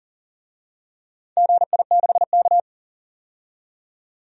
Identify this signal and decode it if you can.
Morse code, heard as a keyed tone, about 40 wpm, 700 Hz. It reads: GI6K
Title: DXCC-100 Call Signs - 4 Characters - 40wpm